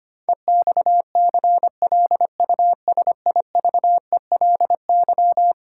Morse code, heard as a keyed tone, about 25 wpm, 700 Hz. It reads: EXCLUSIVELY
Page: morse-code-ninja-practice-25wpm